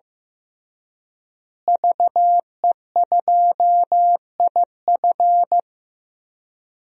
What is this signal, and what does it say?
Morse code, heard as a keyed tone, about 15 wpm, 700 Hz. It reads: VE2IF
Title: DXCC-100 Call Signs - 5 Characters - Repeated 3x - 15wpm